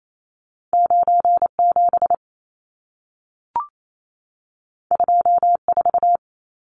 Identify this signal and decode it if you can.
Morse code, heard as a keyed tone, about 28 wpm, 700 Hz. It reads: 97 24
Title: Numbers - 2-Digit 28wpm